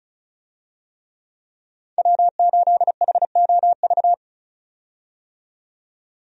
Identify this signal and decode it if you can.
Morse code, heard as a keyed tone, about 36 wpm, 700 Hz. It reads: W8HOV
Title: USA Call Signs - 5 Characters - 35wpm